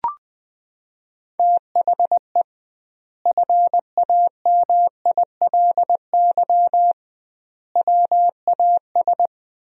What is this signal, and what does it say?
Morse code, heard as a keyed tone, about 20 wpm, 700 Hz. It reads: THE FAMILY WAS
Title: 2x Word-Spacing - Sentences from Top 400 Words - 20wpm